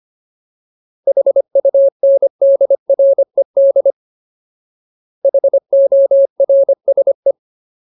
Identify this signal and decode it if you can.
Morse code, heard as a keyed tone, about 25 wpm, 550 Hz. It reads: HUNDRED HORSE
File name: Ultimate Word List / 25WPM Ultimate